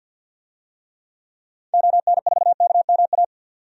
Morse code, heard as a keyed tone, about 46 wpm, 700 Hz. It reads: ON4KMA
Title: DXCC-100 Call Signs - 6 Characters - 50wpm